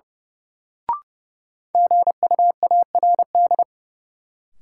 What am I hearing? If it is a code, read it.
Morse code, heard as a keyed tone, about 31 wpm, 700 Hz. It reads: GUARD
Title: Sentences from Top 700 Words - New Word Drill - 30wpm